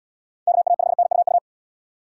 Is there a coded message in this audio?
Yes: Morse code from 0.5 to 1.4 s, heard as a keyed tone, about 74 wpm, 700 Hz. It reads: KA5NUU